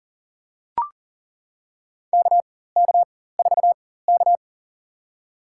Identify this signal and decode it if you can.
Morse code, heard as a keyed tone, about 37 wpm, spaced as for 18 wpm, 700 Hz. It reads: KK4K
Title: USA Call Signs - 4x Character Spacing - 40wpm